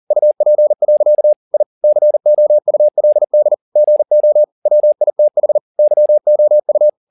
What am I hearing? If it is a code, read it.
Morse code, heard as a keyed tone, about 40 wpm, 600 Hz. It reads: UP. I COULD GO WITH YOU